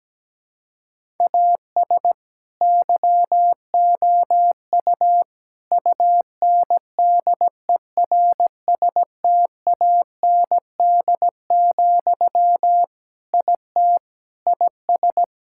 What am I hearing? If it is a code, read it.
Morse code, heard as a keyed tone, about 17 wpm, 700 Hz. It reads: AS YOU UNDERSTAND, IT IS
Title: Sentences from Top 1000 Words - Review Part 5 - 17wpm